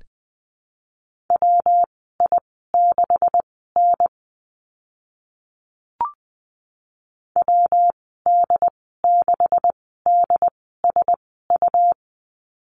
Morse code, heard as a keyed tone, about 20 wpm, spaced as for 14 wpm, 700 Hz. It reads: WI6N WD6DSU
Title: 173 Lesson 28g - Number 9 - Call Signs 20wpm